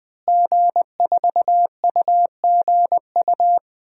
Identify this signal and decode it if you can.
Morse code, heard as a keyed tone, about 20 wpm, 700 Hz. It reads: G4UGU
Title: DXCC-100 Call Signs - 5 Characters - Repeated 2x - 20wpm